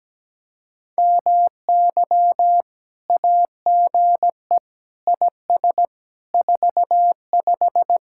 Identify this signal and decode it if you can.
Morse code, heard as a keyed tone, about 17 wpm, 700 Hz. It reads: MY AGE IS 45
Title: Speed-Racing Common QSO Phrases - 17wpm